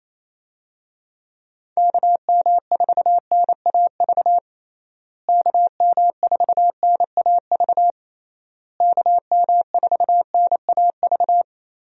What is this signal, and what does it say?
Morse code, heard as a keyed tone, about 28 wpm, 700 Hz. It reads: KM4NAV KM4NAV KM4NAV
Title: USA Call Signs - 6 Characters - Repeated 3x - 28wpm